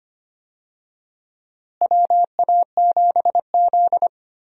Morse code, heard as a keyed tone, about 25 wpm, 700 Hz. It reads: WA7Z
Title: USA Call Signs - 4 Characters - Repeated 2x - 25wpm